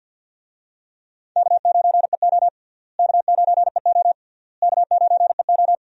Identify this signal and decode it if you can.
Morse code, heard as a keyed tone, about 51 wpm, 700 Hz. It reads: K9EO K9EO K9EO
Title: USA Call Signs - 4 Characters - Repeated 3x - 50wpm